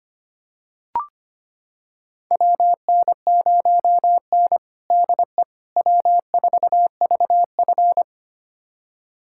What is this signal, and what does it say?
Morse code, heard as a keyed tone, about 25 wpm, 700 Hz. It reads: WN0N DE W4VF
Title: USA Callsign DE Callsign - 4 Characters - 25wpm